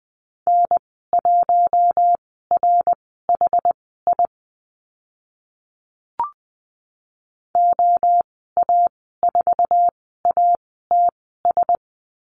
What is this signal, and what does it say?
Morse code, heard as a keyed tone, about 20 wpm, spaced as for 14 wpm, 700 Hz. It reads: N1RHI OA4ATS
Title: 069 Lesson 12g - Letter d - Call Signs 20wpm